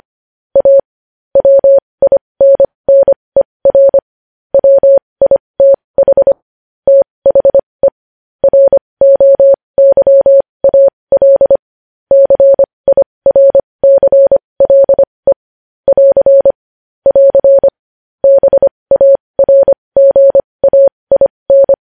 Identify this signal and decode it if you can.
Morse code, heard as a keyed tone, about 25 wpm, spaced as for 20 wpm, 550 Hz. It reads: A WINNER WITH THE ROYAL CIRCLE <AR> <AR> BARGAIN
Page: News Headlines in Morse Code at 20 WPM